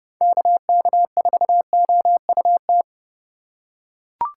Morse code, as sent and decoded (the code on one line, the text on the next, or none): -.- -.- ....- --- ..- -
KK4OUT